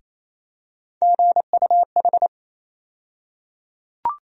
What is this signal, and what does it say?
Morse code, heard as a keyed tone, about 28 wpm, 700 Hz. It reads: GUH
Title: USA Call Sign Postfixes - 3 Letters 28wpm